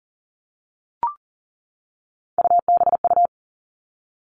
Morse code, heard as a keyed tone, about 39 wpm, 700 Hz. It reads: UBU
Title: Three Challenging Letter-Number Combinations 40wpm